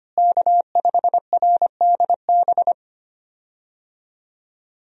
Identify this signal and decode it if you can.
Morse code, heard as a keyed tone, about 25 wpm, 700 Hz. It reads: K5RDB
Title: USA Call Signs - 5 Characters - Repeated 3x - 25wpm